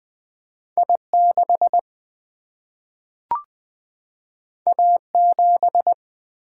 Morse code, heard as a keyed tone, about 20 wpm, 700 Hz. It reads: I6 A7
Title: Single-Letter Single-Number 20wpm